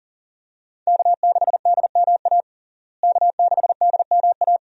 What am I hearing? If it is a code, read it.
Morse code, heard as a keyed tone, about 40 wpm, 700 Hz. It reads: K6DMA K6DMA
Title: USA Call Signs - 5 Characters - Repeated 2x - 40wpm